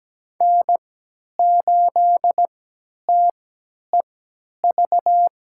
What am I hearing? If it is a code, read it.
Morse code, heard as a keyed tone, about 17 wpm, spaced as for 10 wpm, 700 Hz. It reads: N8TEV
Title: USA Call Signs - 5 Characters - 3x Character Spacing - 17wpm